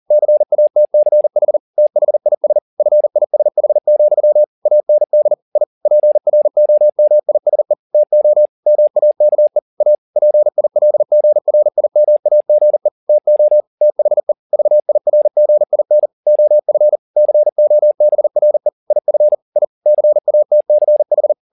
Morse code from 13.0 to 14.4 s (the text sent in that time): TO THE